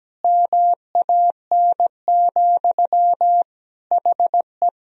Morse code, heard as a keyed tone, about 17 wpm, 700 Hz. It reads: MAN, HE
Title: Sentences from Top 1000 Words - Review Part 2 - 17wpm